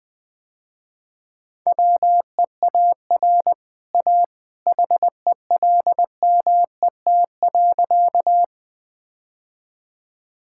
Morse code, heard as a keyed tone, about 20 wpm, 700 Hz. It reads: WEAR A HELMET.
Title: Easy Sentences with Hints - 20wpm